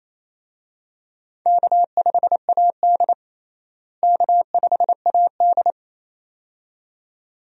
Morse code, heard as a keyed tone, about 28 wpm, 700 Hz. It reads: K5AD K5AD
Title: USA Call Signs - 4 Characters - Repeated 2x - 28wpm